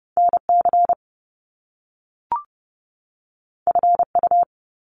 Morse code, heard as a keyed tone, about 30 wpm, 700 Hz. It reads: NC FU
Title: Top Two Letter Combinations in Words 30wpm